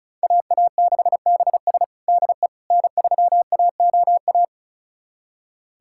Morse code, heard as a keyed tone, about 36 wpm, 700 Hz. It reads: AA6BS DE N3AOA
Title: USA Callsign DE Callsign - 5 Characters - 35wpm